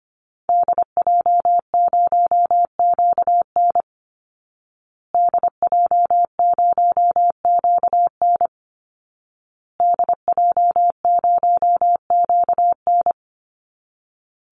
Morse code, heard as a keyed tone, about 25 wpm, 700 Hz. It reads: DJ0QN DJ0QN DJ0QN
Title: Call Signs Repeated 2x - 25wpm